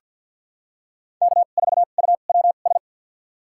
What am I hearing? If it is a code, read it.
Morse code, heard as a keyed tone, about 48 wpm, spaced as for 35 wpm, 700 Hz. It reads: K4UWS
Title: USA Call Signs - 5 Characters - 2x Character Spacing - 50wpm